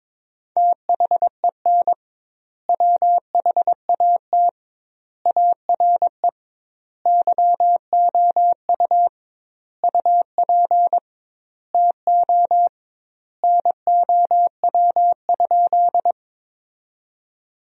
Morse code, heard as a keyed tone, about 22 wpm, 700 Hz. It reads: THEN WHAT ARE YOU UP TO NOW?
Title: 2x Word-Spacing - Sentences from Top 100 Words - 22wpm